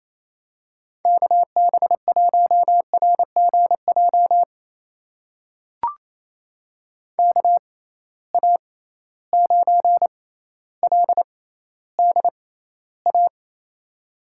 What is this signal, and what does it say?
Morse code, heard as a keyed tone, about 28 wpm, spaced as for 14 wpm, 700 Hz. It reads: KB1RGJ KA9LDA
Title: USA Call Signs - 6 Characters - 6x Character Spacing - 28wpm